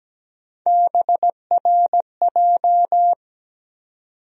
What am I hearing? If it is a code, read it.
Morse code, heard as a keyed tone, about 17 wpm, 700 Hz. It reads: BRJ
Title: USA Call Sign Postfixes - 3 Letters 17wpm